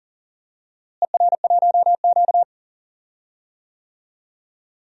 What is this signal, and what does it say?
Morse code, heard as a keyed tone, about 40 wpm, 700 Hz. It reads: ER1Q
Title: DXCC-100 Call Signs - 4 Characters - Repeated 3x - 40wpm